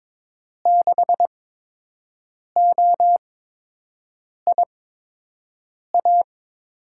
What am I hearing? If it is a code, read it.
Morse code, heard as a keyed tone, about 21 wpm, spaced as for 6 wpm, 700 Hz. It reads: 6OIA
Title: USA Call Signs - 6 Characters - 8x Character Spacing - 22wpm